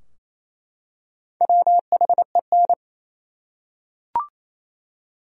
Morse code, heard as a keyed tone, about 28 wpm, 700 Hz. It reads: WHEN
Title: Top 100 Words 28wpm